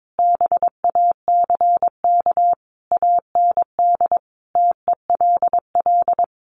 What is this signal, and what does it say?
Morse code, heard as a keyed tone, about 22 wpm, 700 Hz. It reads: BACK AND TELL
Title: Sentences from Top 1100 Words 22wpm